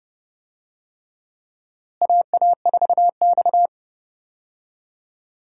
Morse code, heard as a keyed tone, about 30 wpm, 700 Hz. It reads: AA4X
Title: USA Call Signs - 4 Characters - Repeated 3x - 30wpm